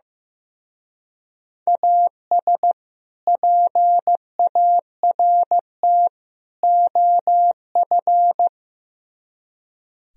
Morse code, heard as a keyed tone, about 16 wpm, 700 Hz. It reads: AS PART OF
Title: Speed-Racing - Sets of 3 Words 15wpm